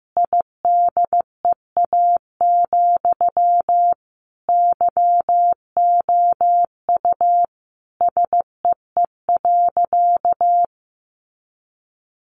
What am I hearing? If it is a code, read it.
Morse code, heard as a keyed tone, about 15 wpm, 700 Hz. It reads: IDEA, YOU SEE.
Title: Sentences from Top 600 Words 15wpm